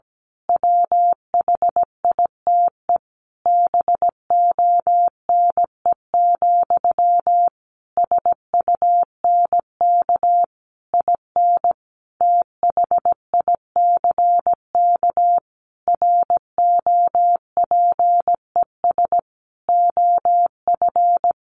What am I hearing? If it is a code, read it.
Morse code, heard as a keyed tone, about 17 wpm, 700 Hz. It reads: WHITE BONE, SUNK IN THICK ROPES OF
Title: A Sound Of Thunder 17wpm